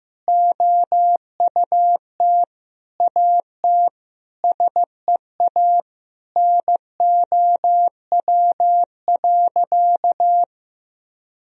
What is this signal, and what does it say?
Morse code, heard as a keyed tone, about 15 wpm, 700 Hz. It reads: OUT AT SEA NOW.
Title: Sentences from Top 1000 Words - Review Part 1 - 15wpm